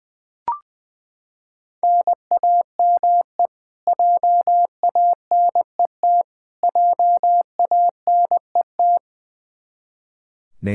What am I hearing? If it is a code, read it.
Morse code, heard as a keyed tone, about 20 wpm, 700 Hz. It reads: NAME JANET JANET
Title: Practice QSO - Exchange 1 - Beginning - 20wpm